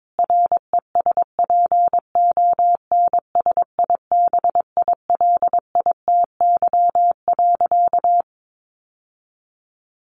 Morse code, heard as a keyed tone, about 22 wpm, 700 Hz. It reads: RESPONSIBILITY.
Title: Sentences from Top 1200 Words 22wpm